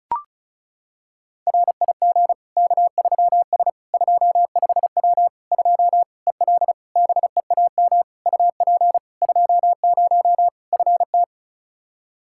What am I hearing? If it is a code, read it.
Morse code, heard as a keyed tone, about 36 wpm, 700 Hz. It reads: RIG K3S 25W 2 EL BEAM UP 20 FT
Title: Practice QSO - Exchange 2 - Info - 35wpm